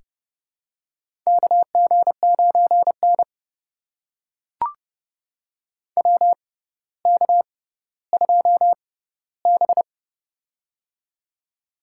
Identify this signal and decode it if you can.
Morse code, heard as a keyed tone, about 30 wpm, spaced as for 15 wpm, 700 Hz. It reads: KG9N WK2B
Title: USA Call Signs - 4 Characters - 6x Character Spacing - 30wpm